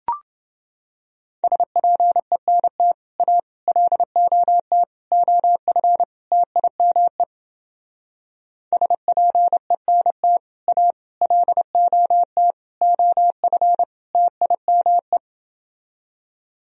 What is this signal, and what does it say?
Morse code, heard as a keyed tone, about 28 wpm, 700 Hz. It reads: SPENT A LOT OF TIME SPENT A LOT OF TIME